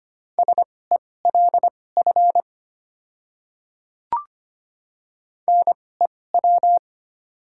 Morse code, heard as a keyed tone, about 25 wpm, spaced as for 18 wpm, 700 Hz. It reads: SELF NEW
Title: 2x Character Spacing - Top 300 Words - 25wpm